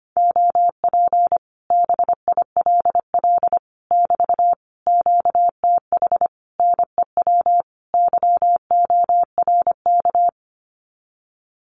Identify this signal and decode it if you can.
Morse code, heard as a keyed tone, about 25 wpm, 700 Hz. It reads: OP BILL <BT> QTH NEW YORK
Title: Common QSO Phrases Encore 25wpm